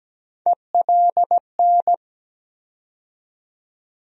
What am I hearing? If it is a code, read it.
Morse code, heard as a keyed tone, about 17 wpm, 700 Hz. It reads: ELN
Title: USA Call Sign Postfixes - 3 Letters 17wpm